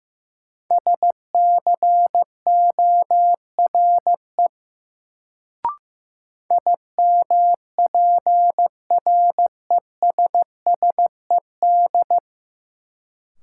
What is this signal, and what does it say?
Morse code, heard as a keyed tone, about 15 wpm, 700 Hz. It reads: SCORE IMPRESSED
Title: Sentences from Top 1600 Words - New Word Drill - 15wpm